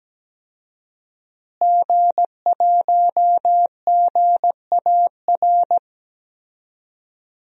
Morse code, heard as a keyed tone, about 17 wpm, 700 Hz. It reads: G1GAR
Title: DXCC-100 Call Signs - 5 Characters - Repeated 2x - 17wpm